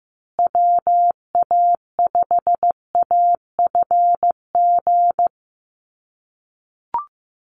Morse code, heard as a keyed tone, about 15 wpm, 700 Hz. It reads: WA5AFG
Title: USA Call Signs Repeated 2x 15wpm